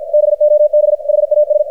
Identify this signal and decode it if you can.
Morse code, heard as a keyed tone, about 51 wpm, 600 Hz, with radio noise. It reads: BOB HAS
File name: archived / CRSnet